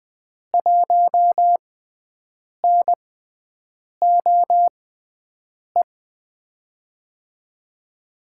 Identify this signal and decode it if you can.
Morse code, heard as a keyed tone, about 20 wpm, spaced as for 7 wpm, 700 Hz. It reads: 1NOE
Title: USA Call Signs - 5 Characters - 6x Character Spacing - 20wpm